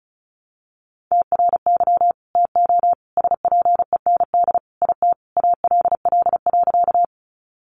Morse code, heard as a keyed tone, about 34 wpm, 700 Hz. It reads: TRY TO SPEND IT ALL.
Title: Sentences from Top 800 Words 35wpm